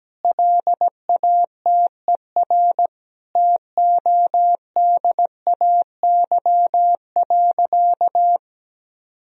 Morse code, heard as a keyed tone, about 17 wpm, 700 Hz. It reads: LATER TODAY.
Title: Sentences from Top 1000 Words - Review Part 4 - 17wpm